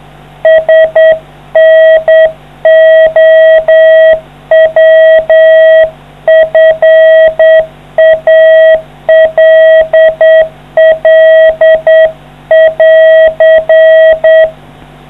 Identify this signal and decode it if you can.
Morse code, heard as a keyed tone, about 8 wpm, 640 Hz. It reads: SNOWFALL<AR>